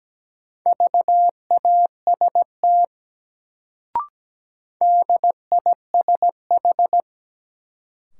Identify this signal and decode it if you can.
Morse code, heard as a keyed tone, about 17 wpm, 700 Hz. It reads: VAST DISH
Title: Sentences from Top 1300 Words - New Word Drill - 17wpm